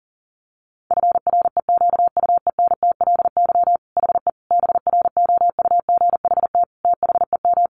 Morse code, heard as a keyed tone, about 37 wpm, 700 Hz. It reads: FREQUENTLY HE BROUGHT THEM
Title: Sentences from Top 1300 Words 40wpm